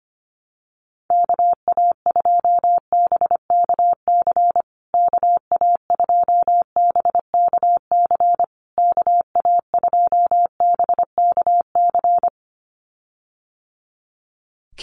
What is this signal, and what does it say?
Morse code, heard as a keyed tone, about 25 wpm, 700 Hz. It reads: KA2BKC KA2BKC KA2BKC
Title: USA Call Signs Repeated 3x 25wpm